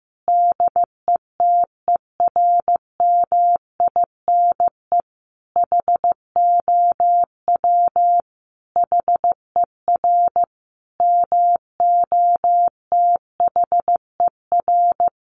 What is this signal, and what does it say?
Morse code, heard as a keyed tone, about 15 wpm, 700 Hz. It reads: DETERMINE HOW HER MOTHER
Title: Sentences from Top 700 Words 15wpm